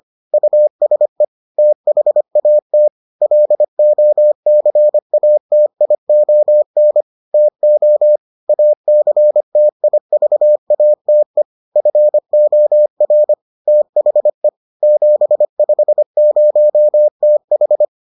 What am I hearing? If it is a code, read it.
Morse code, heard as a keyed tone, about 25 wpm, 600 Hz. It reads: USE THAT LOCATION TO ACTIVATE FOR THE 750TH